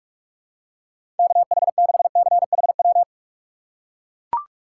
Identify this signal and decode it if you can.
Morse code, heard as a keyed tone, about 41 wpm, 700 Hz. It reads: KH6CHW
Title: USA Call Signs - 6 Characters - 3x Character Spacing - 45wpm